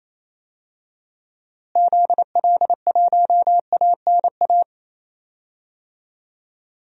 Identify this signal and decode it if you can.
Morse code, heard as a keyed tone, about 28 wpm, 700 Hz. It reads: ZL1ANA
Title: DXCC-100 Call Signs - 6 Characters - Repeated 3x - 28wpm